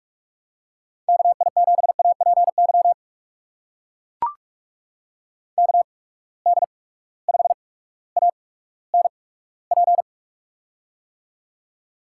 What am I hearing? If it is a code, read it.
Morse code, heard as a keyed tone, about 44 wpm, spaced as for 19 wpm, 700 Hz. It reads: KI7APY KD5ANP
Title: USA Call Signs - 6 Characters - 8x Character Spacing - 45wpm